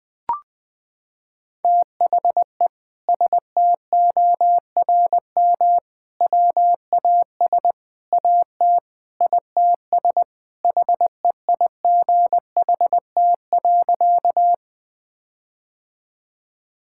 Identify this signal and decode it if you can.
Morse code, heard as a keyed tone, about 20 wpm, 700 Hz. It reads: THE STORM WAS AT ITS HEIGHT.
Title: Sentences from Top 1600 Words 20wpm